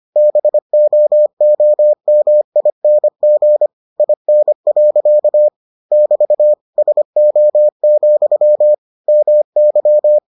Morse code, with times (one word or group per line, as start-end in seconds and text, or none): 0.2-3.7 BOOMING
4.0-5.5 IN.
5.9-6.5 <BT>
6.8-8.7 SO,
9.1-10.2 MY